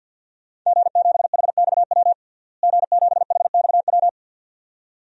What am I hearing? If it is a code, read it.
Morse code, heard as a keyed tone, about 53 wpm, 700 Hz. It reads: G7HXW G7HXW
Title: DXCC-100 Call Signs - 5 Characters - Repeated 2x - 50wpm